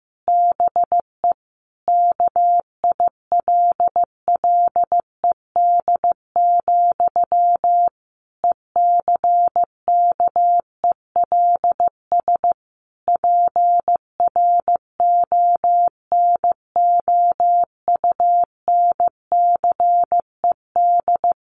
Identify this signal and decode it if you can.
Morse code, heard as a keyed tone, about 14 wpm, 700 Hz. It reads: BE KILLED, ECKELS PRONOUNCED
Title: A Sound Of Thunder 15wpm